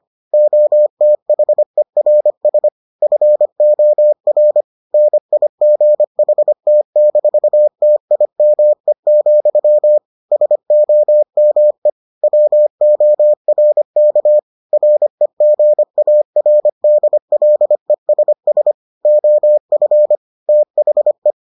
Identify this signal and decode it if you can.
Morse code, heard as a keyed tone, about 25 wpm, 600 Hz. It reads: OTHERS FOR NIGHT-TIME, SOME WORK REGARDLESS OF THE